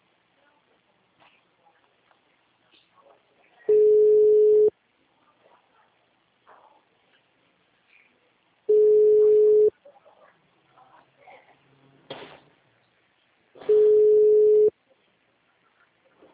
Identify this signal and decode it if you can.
Ringback tone (Europe, 425 Hz), with speech.